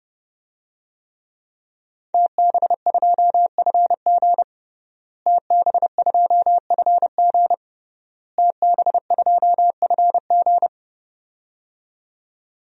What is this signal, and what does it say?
Morse code, heard as a keyed tone, about 30 wpm, 700 Hz. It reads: TB2FG TB2FG TB2FG
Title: DXCC-100 Call Signs - 5 Characters - Repeated 3x - 30wpm